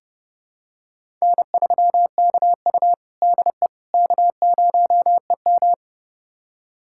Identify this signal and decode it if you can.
Morse code, heard as a keyed tone, about 32 wpm, 700 Hz. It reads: N3KU DE K0EM
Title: USA Callsign DE Callsign - 4 Characters - 30wpm